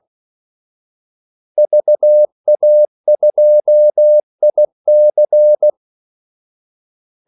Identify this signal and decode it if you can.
Morse code, heard as a keyed tone, about 16 wpm, 600 Hz. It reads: VA2IC
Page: Kaldesignaler 16 wpm